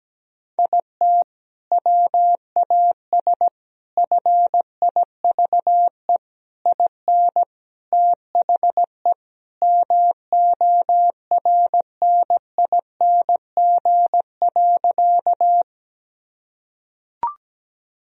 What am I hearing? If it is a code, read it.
Morse code, heard as a keyed tone, about 17 wpm, 700 Hz. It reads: IT WAS FIVE IN THE MORNING.
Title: Sentences from Top 1000 Words - Review Part 2 - 17wpm